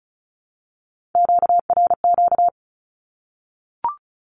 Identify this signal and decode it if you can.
Morse code, heard as a keyed tone, about 34 wpm, 700 Hz. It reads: QRQ
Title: Q-Codes 35wpm